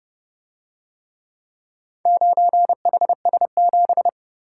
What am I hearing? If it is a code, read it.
Morse code, heard as a keyed tone, about 29 wpm, 700 Hz. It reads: 9HS7